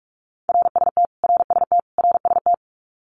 Morse code, heard as a keyed tone, about 44 wpm, 700 Hz. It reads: RST RST RST
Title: Abbreviations Repeated 3x - 45wpm